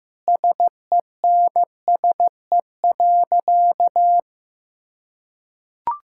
Morse code, heard as a keyed tone, about 15 wpm, 700 Hz. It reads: SENSE.
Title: Sentences from Top 1000 Words - Review Part 6 - 15wpm